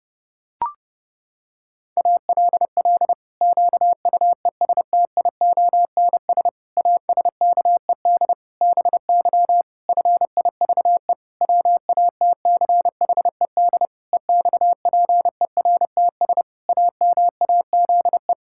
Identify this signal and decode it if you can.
Morse code, heard as a keyed tone, about 30 wpm, 700 Hz. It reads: ALL QUESTIONS ASKED BY FIVE WATCHED EXPERTS AMAZE
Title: Pangram Challenge v2 - 30wpm